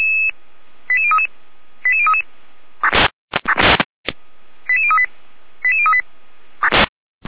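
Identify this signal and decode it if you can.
Five-tone selective call (ZVEI), heard as quick runs of five tones, tones of about 70 ms. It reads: 88033 88033 88038 88038
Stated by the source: Five-tone ZVEI selective call received on an SDR receiver.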